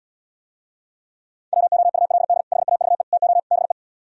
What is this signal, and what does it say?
Morse code, heard as a keyed tone, about 77 wpm, 700 Hz. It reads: WOULD HAVE TO BE